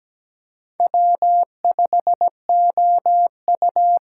Morse code, heard as a keyed tone, about 17 wpm, 700 Hz. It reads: W5OU
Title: USA Call Signs - 4x Character Spacing - 17wpm